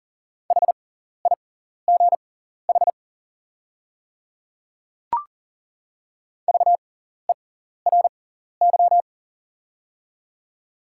Morse code, heard as a keyed tone, about 40 wpm, spaced as for 14 wpm, 700 Hz. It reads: HIGH VERY
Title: 6x Character Spacing - Top 300 Words - 40wpm